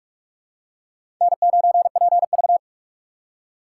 Morse code, heard as a keyed tone, about 44 wpm, 700 Hz. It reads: N9PV
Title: USA Call Signs - 4 Characters - 3x Character Spacing - 45wpm